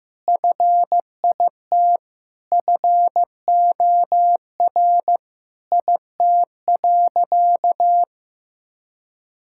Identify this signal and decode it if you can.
Morse code, heard as a keyed tone, about 15 wpm, 700 Hz. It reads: FIT FOR IT.
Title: Sentences from Top 1000 Words - Review Part 5 - 15wpm